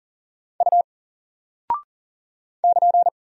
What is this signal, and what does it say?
Morse code, heard as a keyed tone, about 38 wpm, 700 Hz. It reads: U <KN>
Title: Single Character (Letter:Number:Punctuation:Prosign) Alt 40wpm